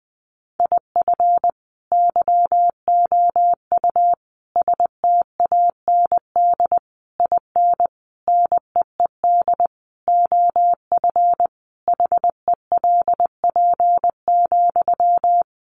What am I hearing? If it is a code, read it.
Morse code, heard as a keyed tone, about 20 wpm, 700 Hz. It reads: IF YOU STAND IN NEED OF HELP,
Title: Sentences from Top 1200 Words 20wpm